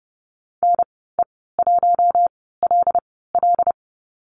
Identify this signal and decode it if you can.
Morse code, heard as a keyed tone, about 30 wpm, spaced as for 17 wpm, 700 Hz. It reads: NE1LL
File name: Lesson 14g - Number 2 - Call Signs 30wpm